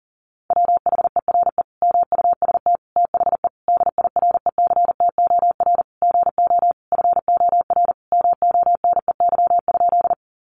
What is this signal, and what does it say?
Morse code, heard as a keyed tone, about 40 wpm, 700 Hz. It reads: WHERE MUST THE DIRECTOR GO FOR MONEY?
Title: Sentences from Top 1300 Words 40wpm